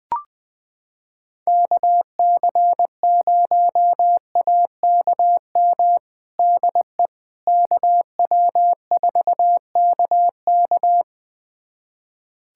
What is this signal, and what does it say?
Morse code, heard as a keyed tone, about 20 wpm, 700 Hz. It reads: KC0AKM DE KW4KK
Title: USA Callsign DE Callsign 20wpm